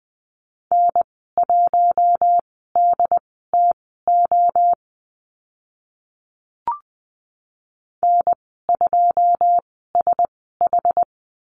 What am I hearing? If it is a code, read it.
Morse code, heard as a keyed tone, about 20 wpm, spaced as for 14 wpm, 700 Hz. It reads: N1DTO N2SH
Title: 082 Lesson 14g - Number 2 - Call Signs 20wpm